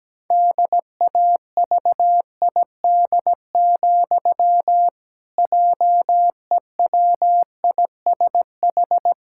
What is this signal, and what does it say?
Morse code, heard as a keyed tone, about 17 wpm, 700 Hz. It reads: DAVID, JEWISH
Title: Pangram Challenge v2 - 17wpm